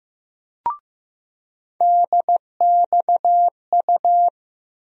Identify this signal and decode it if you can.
Morse code, heard as a keyed tone, about 15 wpm, 700 Hz. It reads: DXU
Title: USA Call Sign Postfixes - 3 Letters 15wpm